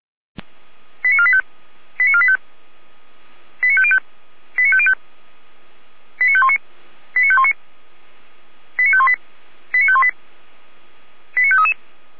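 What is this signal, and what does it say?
Five-tone selective call (ZVEI), heard as quick runs of five tones, tones of about 70 ms. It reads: 89485 89485 89505 89505 89519 89519 89518 89518 89533
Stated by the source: ZVEI five-tone selective call received on an SDR.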